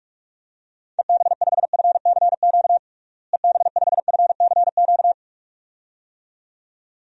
Morse code, heard as a keyed tone, about 46 wpm, 700 Hz. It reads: EB5FCQ EB5FCQ
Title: DXCC-100 Call Signs - 6 Characters - Repeated 2x - 45wpm